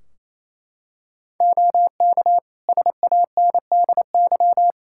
Morse code, heard as a keyed tone, about 28 wpm, 700 Hz. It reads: OK SANDY
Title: Practice QSO - Exchange 3 - More Info - 28wpm